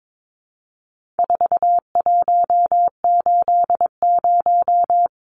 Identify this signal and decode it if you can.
Morse code, heard as a keyed tone, about 22 wpm, 700 Hz. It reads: 4180
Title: Numbers - 4-Digit 22wpm